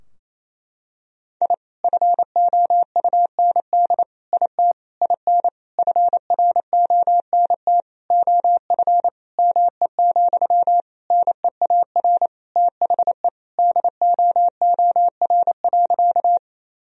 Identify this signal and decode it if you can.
Morse code, heard as a keyed tone, about 28 wpm, 700 Hz. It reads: I FOUND IT IN FRONT OF ME, NEAR THE DOOR.
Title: Sentences from Top 1000 Words - Review Part 2 - 28wpm